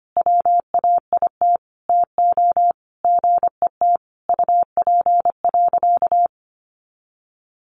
Morse code, heard as a keyed tone, about 25 wpm, 700 Hz. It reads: WAIT TO GET UP.
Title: Sentences from Top 500 Words 25wpm